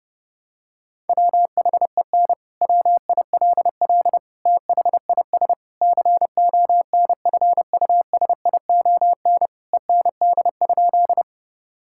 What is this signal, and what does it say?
Morse code, heard as a keyed tone, about 30 wpm, 700 Hz. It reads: WHEN WILL THIS CONFUSION END?
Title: Sentences from Top 1600 Words 30wpm